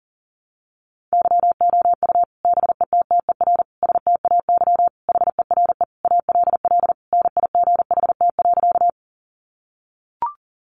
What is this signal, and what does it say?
Morse code, heard as a keyed tone, about 40 wpm, 700 Hz. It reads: YOU BETTER STAY HERE ALL NIGHT.
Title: Sentences from Top 500 Words 40wpm